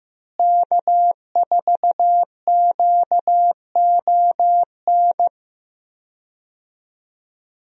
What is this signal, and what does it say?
Morse code, heard as a keyed tone, about 15 wpm, 700 Hz. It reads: K4QON
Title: USA Call Signs - 5 Characters - Repeated 3x - 15wpm